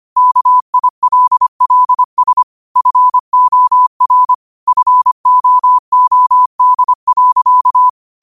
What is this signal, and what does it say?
Morse code, heard as a keyed tone, about 25 wpm, 1000 Hz. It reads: KILLS FOR FOOD.
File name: qotdfiles / 25wpm